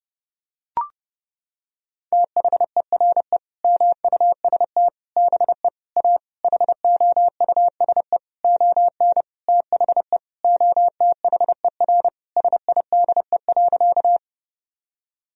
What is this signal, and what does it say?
Morse code, heard as a keyed tone, about 32 wpm, 700 Hz. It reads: THERE MUST BE A HOUSE ON THE OTHER SIDE.
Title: Sentences from Top 1000 Words - Review Part 1 - 30wpm